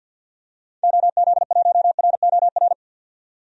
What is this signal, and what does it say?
Morse code, heard as a keyed tone, about 50 wpm, 700 Hz. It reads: OZ1ROR
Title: DXCC-100 Call Signs - 6 Characters - 50wpm